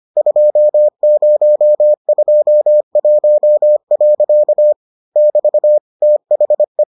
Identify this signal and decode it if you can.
Morse code, heard as a keyed tone, about 25 wpm, 600 Hz. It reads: 2021. <BT> THE